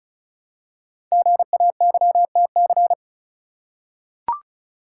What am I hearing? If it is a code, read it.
Morse code, heard as a keyed tone, about 36 wpm, 700 Hz. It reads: GAYTC
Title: Code Groups - 5 Letters - 35wpm